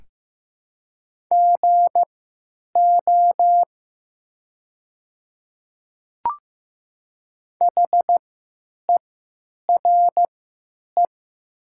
Morse code, heard as a keyed tone, about 15 wpm, spaced as for 9 wpm, 700 Hz. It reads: GO HERE
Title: 3x Character Spacing - Top 500 Words - 15wpm